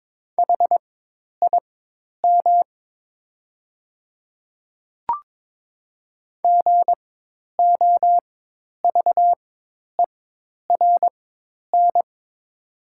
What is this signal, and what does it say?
Morse code, heard as a keyed tone, about 22 wpm, spaced as for 10 wpm, 700 Hz. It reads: HIM GOVERN
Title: 4x Character Spacing - Top 500 Words - 22wpm